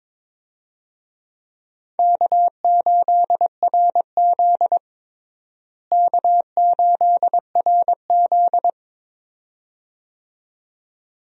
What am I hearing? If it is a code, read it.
Morse code, heard as a keyed tone, about 22 wpm, 700 Hz. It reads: K8RZ K8RZ
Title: USA Call Signs - 4 Characters - Repeated 2x - 22wpm